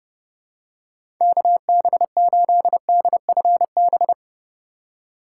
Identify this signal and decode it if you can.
Morse code, heard as a keyed tone, about 32 wpm, 700 Hz. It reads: KB8DFB
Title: USA Call Signs - 6 Characters - 2x Character Spacing - 30wpm